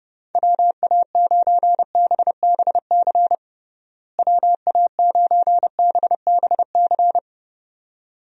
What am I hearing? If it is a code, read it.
Morse code, heard as a keyed tone, about 30 wpm, 700 Hz. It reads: WA9BBC WA9BBC
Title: USA Call Signs - 6 Characters - Repeated 2x - 30wpm